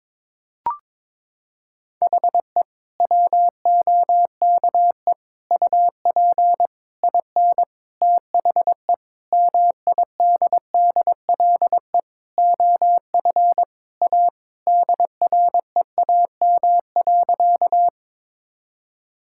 Morse code, heard as a keyed tone, about 22 wpm, 700 Hz. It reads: HE WOKE UP IN THE MIDDLE OF A DREAM.
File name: practice / 22wpm / Sentences with 4-Word Phrases - 22wpm